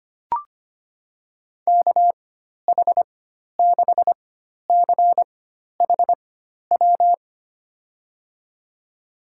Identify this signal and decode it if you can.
Morse code, heard as a keyed tone, about 25 wpm, spaced as for 12 wpm, 700 Hz. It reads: KH6CHW
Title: USA Call Signs - 6 Characters - 4x Character Spacing - 25wpm